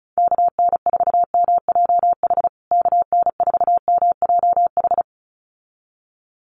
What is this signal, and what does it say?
Morse code, heard as a keyed tone, about 34 wpm, 700 Hz. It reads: KN4MJH KN4MJH
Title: USA Call Signs Repeated 2x 35wpm